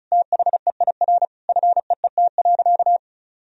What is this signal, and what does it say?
Morse code, heard as a keyed tone, about 36 wpm, 700 Hz. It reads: THEIR FEET.
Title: Sentences from Top 1000 Words - Review Part 1 - 35wpm